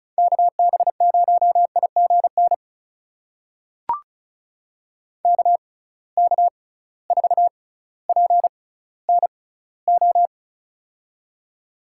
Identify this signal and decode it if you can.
Morse code, heard as a keyed tone, about 34 wpm, spaced as for 18 wpm, 700 Hz. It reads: KB0IGN KK4PNO
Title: USA Call Signs - 6 Characters - 6x Character Spacing - 35wpm